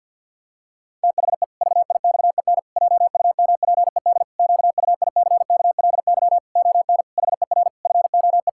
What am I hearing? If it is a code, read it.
Morse code, heard as a keyed tone, about 49 wpm, 700 Hz. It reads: THE VIXEN JUMPED QUICKLY ON HER FOE
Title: Pangram Challenge v2 - 50wpm